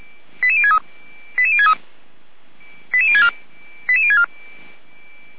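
Five-tone selective call (ZVEI), heard as quick runs of five tones, tones of about 70 ms. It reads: 88063 88063 88064 88064
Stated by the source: ZVEI five-tone selective call received via SDR.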